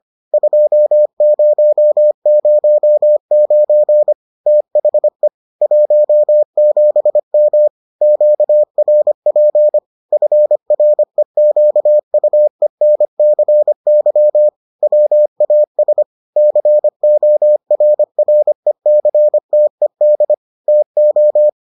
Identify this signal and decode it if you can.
Morse code, heard as a keyed tone, about 25 wpm, 600 Hz. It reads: 2009 THE 17M QRP FREQUENCY WAS CORRECTED TO